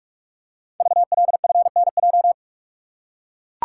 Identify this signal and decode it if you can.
Morse code, heard as a keyed tone, about 44 wpm, 700 Hz. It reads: VLFNJ
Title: Code Groups - 5 Letters - 45wpm